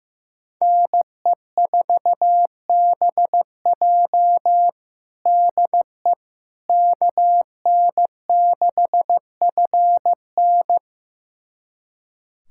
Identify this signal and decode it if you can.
Morse code, heard as a keyed tone, about 15 wpm, 700 Hz. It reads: NE4BJ DE KN6FN
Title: USA Callsign DE Callsign - 5 Characters - 15wpm